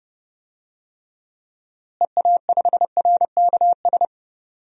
Morse code, heard as a keyed tone, about 30 wpm, 700 Hz. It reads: EA5RKS
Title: DXCC-100 Call Signs - 6 Characters - Repeated 2x - 30wpm